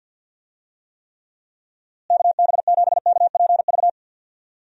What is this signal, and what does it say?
Morse code, heard as a keyed tone, about 51 wpm, 700 Hz. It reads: KB7KPV
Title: USA Call Signs - 6 Characters - Repeated 3x - 50wpm